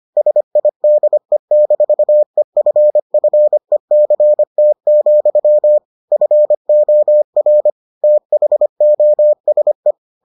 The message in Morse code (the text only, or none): SIDE-EFFECT, FOR THOSE